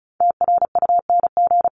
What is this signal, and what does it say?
Morse code, heard as a keyed tone, about 34 wpm, 700 Hz. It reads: TRUNG
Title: CWT Exchange 35wpm